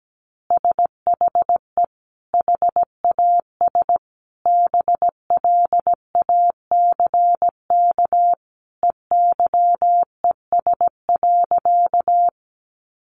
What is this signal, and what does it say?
Morse code, heard as a keyed tone, about 17 wpm, 700 Hz. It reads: SHE HAS BLACK EYES.
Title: Easy Phrases 17wpm